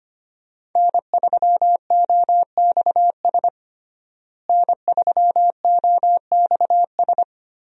Morse code, heard as a keyed tone, about 25 wpm, 700 Hz. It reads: N3OXS N3OXS
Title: USA Call Signs - 5 Characters - Repeated 2x - 25wpm